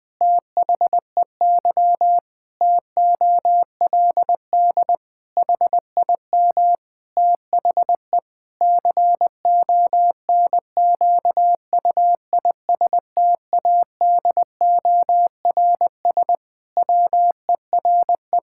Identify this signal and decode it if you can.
Morse code, heard as a keyed tone, about 20 wpm, 700 Hz. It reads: THEY TOLD HIM THE CONQUISTADORS WERE
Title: Pangram Challenge v2 - 20wpm